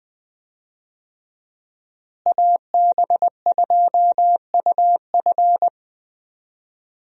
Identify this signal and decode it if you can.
Morse code, heard as a keyed tone, about 20 wpm, 700 Hz. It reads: AB2UF
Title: USA Call Signs - 5 Characters - Repeated 2x - 20wpm